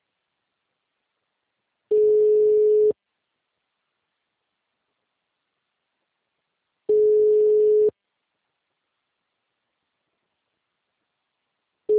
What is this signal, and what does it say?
Ringback tone (Europe, 425 Hz).
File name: out- -2048- - - .64